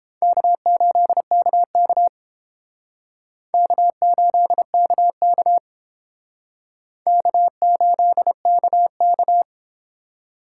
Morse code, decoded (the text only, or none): K8KK K8KK K8KK